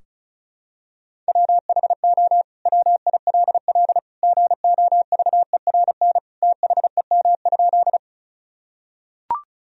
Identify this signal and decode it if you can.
Morse code, heard as a keyed tone, about 34 wpm, 700 Hz. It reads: WHO WILL GOVERN THEM?
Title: Sentences from Top 1000 Words - Review Part 2 - 35wpm